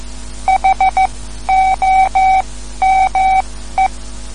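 Morse code, heard as a keyed tone, about 14 wpm, spaced as for 12 wpm, 760 Hz. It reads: HOME
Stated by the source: Morse code lesson, third lesson in a series for learning the Morse alphabet.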